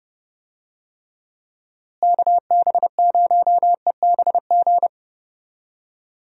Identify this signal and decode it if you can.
Morse code, heard as a keyed tone, about 30 wpm, 700 Hz. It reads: KB0EBG
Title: USA Call Signs - 6 Characters - Repeated 2x - 30wpm